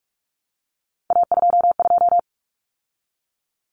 Morse code, heard as a keyed tone, about 44 wpm, 700 Hz. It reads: A22
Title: Common Cut Numbers - 3-Digit 45wpm